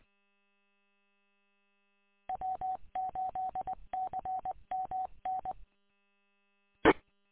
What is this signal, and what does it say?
Morse code, heard as a keyed tone, about 21 wpm, 750 Hz. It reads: W8CMN E
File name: w8cmn- -combined